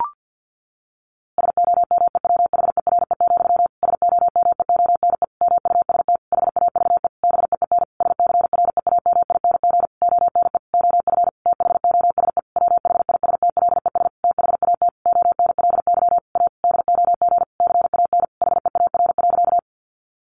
Morse code, heard as a keyed tone, about 50 wpm, 700 Hz. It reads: SOMEWHERE, SOMEONE MUST HAVE BEEN SCREAMING ONE OF THOSE WHISTLES THAT ONLY A DOG CAN HEAR.
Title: A Sound Of Thunder 50wpm